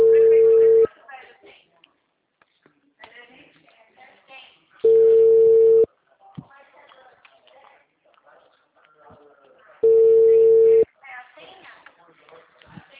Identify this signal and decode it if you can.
Ringback tone (China, 450 Hz), with speech.
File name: out- -2005- - - .78998